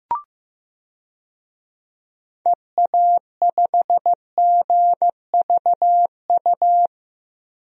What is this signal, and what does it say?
Morse code, heard as a keyed tone, about 16 wpm, 700 Hz. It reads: EA5GVU
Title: DXCC-100 Call Signs - 6 Characters - 15wpm